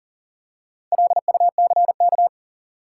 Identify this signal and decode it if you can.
Morse code, heard as a keyed tone, about 40 wpm, 700 Hz. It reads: LUCK
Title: Four Letter Words - 40wpm